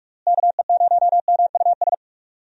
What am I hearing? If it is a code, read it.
Morse code, heard as a keyed tone, about 42 wpm, 700 Hz. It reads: KE0MUS
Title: USA Call Signs - 6 Characters - Repeated 3x - 45wpm